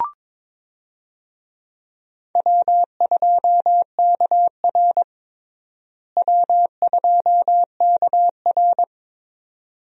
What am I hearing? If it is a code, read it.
Morse code, heard as a keyed tone, about 22 wpm, 700 Hz. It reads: W2KR W2KR
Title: USA Call Signs - 4 Characters - Repeated 2x - 22wpm